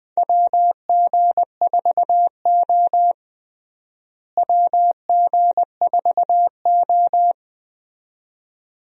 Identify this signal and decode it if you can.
Morse code, heard as a keyed tone, about 20 wpm, 700 Hz. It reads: WG4O WG4O
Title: USA Call Signs - 4 Characters - Repeated 2x - 20wpm